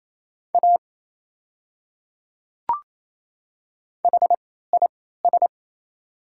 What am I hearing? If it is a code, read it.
Morse code, heard as a keyed tone, about 28 wpm, spaced as for 16 wpm, 690 Hz. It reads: A HIS
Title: 3x Character Spacing - Top 100 Words - 28wpm